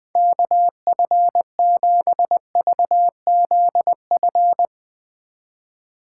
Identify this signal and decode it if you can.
Morse code, heard as a keyed tone, about 20 wpm, 700 Hz. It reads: KF7VZF
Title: Speed Racing - USA Call Signs 17wpm